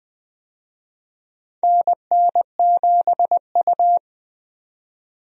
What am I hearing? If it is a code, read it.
Morse code, heard as a keyed tone, about 21 wpm, 700 Hz. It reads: NN7U
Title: USA Call Signs - 4 Characters - 20wpm